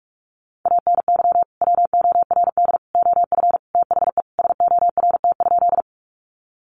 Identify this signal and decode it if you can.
Morse code, heard as a keyed tone, about 44 wpm, 700 Hz. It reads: ANY WORD OF THE SORT?
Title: Sentences from Top 1300 Words 45wpm